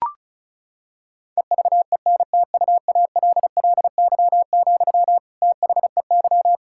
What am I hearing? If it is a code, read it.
Morse code, heard as a keyed tone, about 34 wpm, 700 Hz. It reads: EVENTUALLY, THEY
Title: Sentences from Top 1000 Words - Review Part 6 - 35wpm